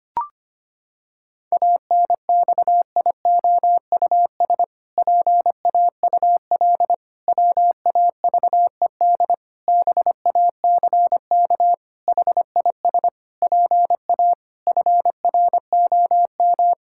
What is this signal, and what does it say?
Morse code, heard as a keyed tone, about 25 wpm, 700 Hz. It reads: ANXIOUS PAUL WAVED BACK HIS PA FROM
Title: Pangram Challenge v2 - 25wpm